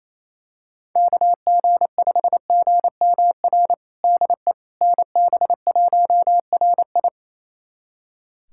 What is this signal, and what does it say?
Morse code, heard as a keyed tone, about 28 wpm, 700 Hz. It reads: KG5GMR DE NB1RI
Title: USA Callsign DE Callsign 28wpm